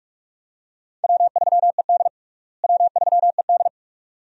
Morse code, heard as a keyed tone, about 41 wpm, 700 Hz. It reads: W3ED W3ED
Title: USA Call Signs - 4 Characters - Repeated 2x - 45wpm